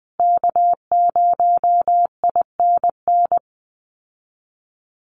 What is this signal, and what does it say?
Morse code, heard as a keyed tone, about 20 wpm, 700 Hz. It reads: K0INN
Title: Call Signs Repeated 2x - 20wpm